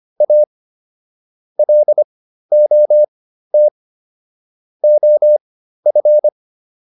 Morse code, heard as a keyed tone, about 25 wpm, spaced as for 13 wpm, 600 Hz. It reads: A LOT OF
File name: WD404 13